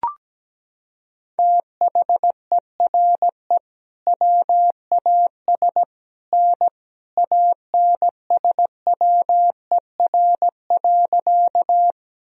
Morse code, heard as a keyed tone, about 17 wpm, 700 Hz. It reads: THERE WAS N ANSWER.
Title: Sentences from Top 300 Words and Abbreviations 17wpm